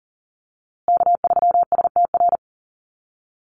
Morse code, heard as a keyed tone, about 40 wpm, 700 Hz. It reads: K3STR
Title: USA Call Signs Repeated 2x 40wpm